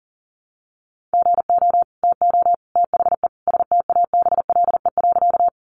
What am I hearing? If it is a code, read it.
Morse code, heard as a keyed tone, about 40 wpm, 700 Hz. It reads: GO TO THE STABLE.
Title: Sentences from Top 1500 Words 40wpm